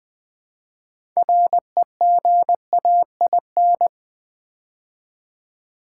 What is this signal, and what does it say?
Morse code, heard as a keyed tone, about 20 wpm, 690 Hz. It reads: REGAIN
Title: Build A Word - 2 Letters At A Time - 6-Letter Words - 20wpm